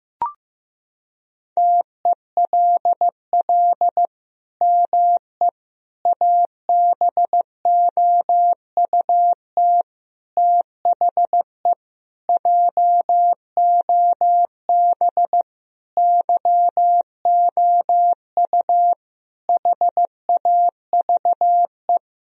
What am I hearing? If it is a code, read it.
Morse code, heard as a keyed tone, about 15 wpm, 700 Hz. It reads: TELL ME ABOUT THE JOB YOU HAVE
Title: Sentences from Top 1000 Words - Review Part 3 - 15wpm